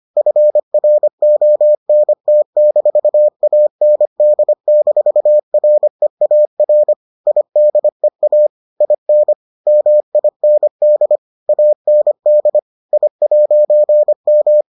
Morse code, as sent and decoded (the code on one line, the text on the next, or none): ..-. .-. --- -. - -....- .- -. -.. -....- .-. .- .-. / .. -.. .- / .. -. / -- .. -. -.. / .- -. -.. / .. .----. --
FRONT-AND-REAR IDEA IN MIND AND I'M